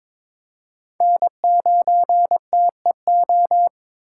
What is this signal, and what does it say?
Morse code, heard as a keyed tone, about 22 wpm, 700 Hz. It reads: N9TEO
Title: USA Call Signs - 8x Character Spacing - 22wpm